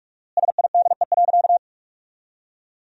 Morse code, heard as a keyed tone, about 46 wpm, 700 Hz. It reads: SIDE.
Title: 4x Word-Spacing - Sentences from Top 400 Words - 45wpm